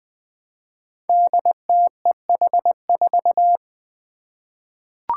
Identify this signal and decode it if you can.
Morse code, heard as a keyed tone, about 20 wpm, 700 Hz. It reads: DTEH4
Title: Five Challenging Letter-Number Combinations 20wpm